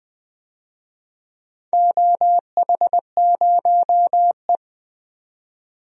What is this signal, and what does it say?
Morse code, heard as a keyed tone, about 20 wpm, 700 Hz. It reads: OH0E
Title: DXCC-100 Call Signs - 4 Characters - 20wpm